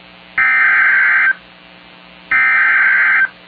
Emergency Alert System (SAME) tones (header bursts), heard as harsh buzzing bursts. It is heard from 0.4 to 3.2 s.